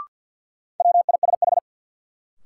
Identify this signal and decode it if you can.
Morse code, heard as a keyed tone, about 48 wpm, 700 Hz. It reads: WISH
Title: Sentences from Top 1600 Words - New Word Drill - 50wpm